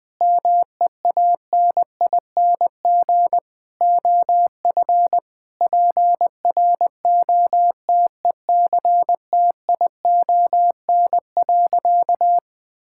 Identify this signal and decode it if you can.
Morse code, heard as a keyed tone, about 20 wpm, 700 Hz. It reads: MEANING OF PROTECTION.
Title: Sentences from Top 1600 Words 20wpm